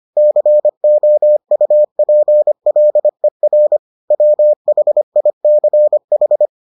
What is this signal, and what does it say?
Morse code, heard as a keyed tone, about 25 wpm, 600 Hz. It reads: COUPLER WHICH